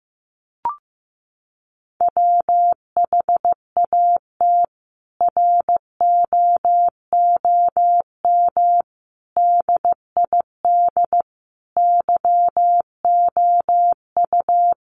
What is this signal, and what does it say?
Morse code, heard as a keyed tone, about 15 wpm, 700 Hz. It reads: WHAT ROOM DID YOU
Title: Sentences from Top 600 Words 15wpm